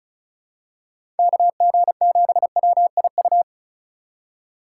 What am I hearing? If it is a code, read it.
Morse code, heard as a keyed tone, about 34 wpm, 700 Hz. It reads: KG7WIU
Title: USA Call Signs - 6 Characters - 2x Character Spacing - 35wpm